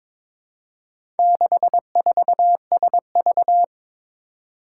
Morse code, heard as a keyed tone, about 22 wpm, 700 Hz. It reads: 64SV
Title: Four Challenging Letter-Number Combinations 22wpm